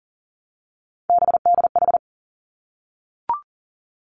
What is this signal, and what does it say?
Morse code, heard as a keyed tone, about 38 wpm, 700 Hz. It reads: BDH
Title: Three Challenging Letter-Number Combinations 40wpm